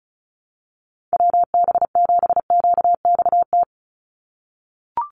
Morse code, heard as a keyed tone, about 34 wpm, 700 Hz. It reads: WB7QXT
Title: USA Call Signs 35wpm